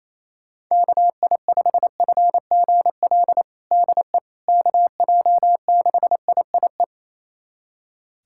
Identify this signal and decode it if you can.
Morse code, heard as a keyed tone, about 28 wpm, 700 Hz. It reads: KI5FGL DE KJ6IIE
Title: USA Callsign DE Callsign - 6 Characters - 28wpm